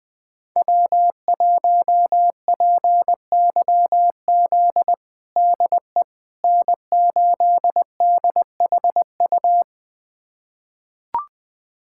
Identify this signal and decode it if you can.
Morse code, heard as a keyed tone, about 20 wpm, 700 Hz. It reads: W1PYZ DE N8DHU
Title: USA Callsign DE Callsign - 5 Characters - 20wpm